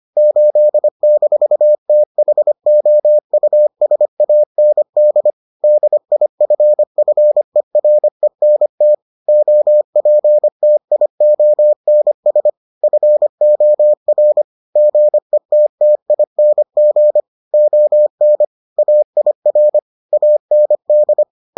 Morse code, heard as a keyed tone, about 25 wpm, 600 Hz. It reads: 8-THOUSAND DIFFERENT OPTIONS FOR GETTING ON AIR AND